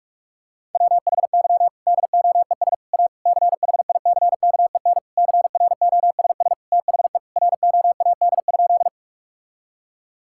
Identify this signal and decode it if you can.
Morse code, heard as a keyed tone, about 46 wpm, 700 Hz. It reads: WHY DOES A CHICKEN CROSS THE ROAD?
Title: Sentences from Top 1000 Words - Review Part 5 - 45wpm